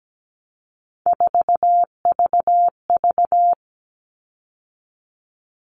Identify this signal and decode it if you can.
Morse code, heard as a keyed tone, about 17 wpm, 700 Hz. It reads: 4VV
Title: All Cut Numbers - 3-Digit 17wpm